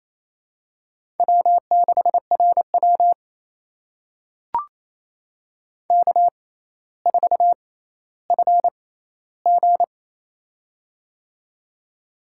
Morse code, heard as a keyed tone, about 28 wpm, spaced as for 14 wpm, 700 Hz. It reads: W6RW K4FG
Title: USA Call Signs - 4 Characters - 6x Character Spacing - 28wpm